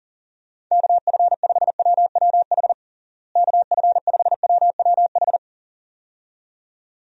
Morse code, heard as a keyed tone, about 40 wpm, 700 Hz. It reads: KF5WWH KF5WWH
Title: USA Call Signs - 6 Characters - Repeated 2x - 40wpm